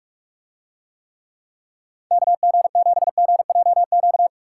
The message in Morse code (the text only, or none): KG7GJQ